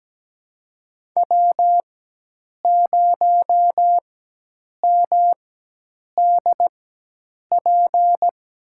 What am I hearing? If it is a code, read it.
Morse code, heard as a keyed tone, about 17 wpm, spaced as for 8 wpm, 700 Hz. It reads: W0MDP
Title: USA Call Signs - 4x Character Spacing - 17wpm